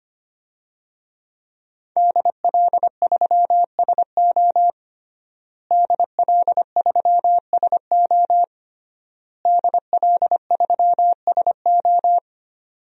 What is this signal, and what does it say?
Morse code, heard as a keyed tone, about 25 wpm, 700 Hz. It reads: DL3SO DL3SO DL3SO
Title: DXCC-100 Call Signs - 5 Characters - Repeated 3x - 25wpm